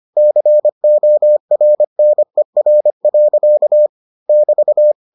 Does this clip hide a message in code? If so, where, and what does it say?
Yes: Morse code from 0.2 to 4.9 s, heard as a keyed tone, about 25 wpm, 600 Hz. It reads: CORNER. <BT>